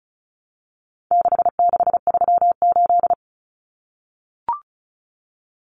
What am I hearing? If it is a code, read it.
Morse code, heard as a keyed tone, about 34 wpm, 700 Hz. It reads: 6638
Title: Numbers - 4-Digit 35wpm